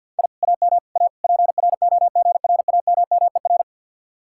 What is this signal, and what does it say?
Morse code, heard as a keyed tone, about 65 wpm, spaced as for 57 wpm, 700 Hz. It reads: I AM A PROGRAMMER
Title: Speed-Racing Common QSO Phrases Encore - 50wpm